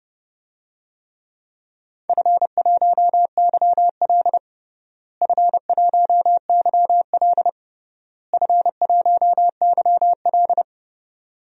Morse code, heard as a keyed tone, about 30 wpm, 700 Hz. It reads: F1YL F1YL F1YL
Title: DXCC-100 Call Signs - 4 Characters - Repeated 3x - 30wpm